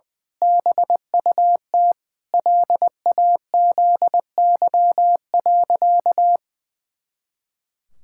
Morse code, heard as a keyed tone, about 20 wpm, 700 Hz. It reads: BUT LAZY.
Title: Pangram Challenge v2 - 20wpm